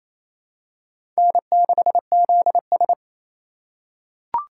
Morse code, heard as a keyed tone, about 28 wpm, 700 Hz. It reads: N6ZS
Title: USA Call Signs - 4x Character Spacing - 28wpm